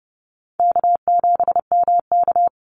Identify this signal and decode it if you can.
Morse code, heard as a keyed tone, about 30 wpm, 700 Hz. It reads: K7MK
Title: NA SOTA Activator Callsigns 30wpm